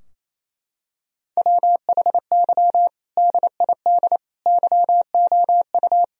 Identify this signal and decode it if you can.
Morse code, heard as a keyed tone, about 28 wpm, 700 Hz. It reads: WHY DID YOU
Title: Sentences from Top 1000 Words - Review Part 3 - 28wpm